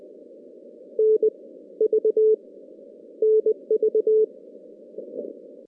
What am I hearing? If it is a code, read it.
Morse code, heard as a keyed tone, about 20 wpm, 440 Hz. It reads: N V NV
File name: contests / CW-SS-k5tr- -2100